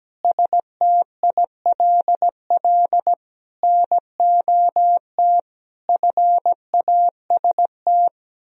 Morse code, heard as a keyed tone, about 17 wpm, 700 Hz. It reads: STILL NOT FAST
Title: Sentences from Top 1000 Words - Review Part 3 - 17wpm